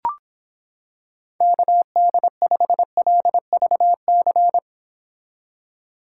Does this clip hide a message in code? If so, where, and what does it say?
Yes: Morse code from 1.4 to 4.6 s, heard as a keyed tone, about 26 wpm, 700 Hz. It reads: KD5LVC